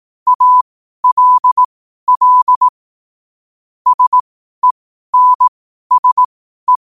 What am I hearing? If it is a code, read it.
Morse code, heard as a keyed tone, about 18 wpm, spaced as for 12 wpm, 1000 Hz. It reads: ALL SENSE
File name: qotdfiles / 12wpm